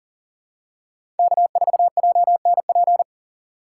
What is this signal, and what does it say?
Morse code, heard as a keyed tone, about 40 wpm, 700 Hz. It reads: K4JNP
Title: USA Call Signs - 5 Characters - 2x Character Spacing - 40wpm